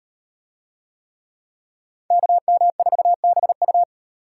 Morse code, heard as a keyed tone, about 38 wpm, 700 Hz. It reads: KM4BU